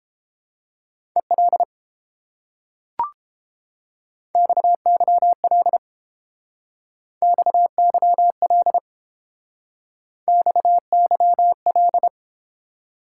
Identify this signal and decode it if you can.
Morse code, heard as a keyed tone, about 30 wpm, 700 Hz. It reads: EL XYL XYL XYL